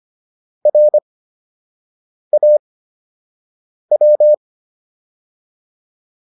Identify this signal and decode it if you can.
Morse code, heard as a keyed tone, about 25 wpm, spaced as for 6 wpm, 600 Hz. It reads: RAW